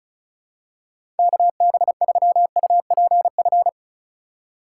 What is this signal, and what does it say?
Morse code, heard as a keyed tone, about 36 wpm, 690 Hz. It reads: KB3UPF
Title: USA Call Signs - 6 Characters - 2x Character Spacing - 35wpm